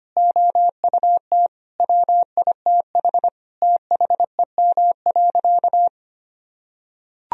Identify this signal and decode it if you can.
Morse code, heard as a keyed tone, about 25 wpm, 700 Hz. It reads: OUT WITH THEM.
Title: Sentences from Top 1000 Words - Review Part 3 - 25wpm